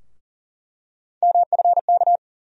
Morse code, heard as a keyed tone, about 40 wpm, 700 Hz. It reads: MFK
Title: USA Call Sign Postfixes - 3 Letters 40wpm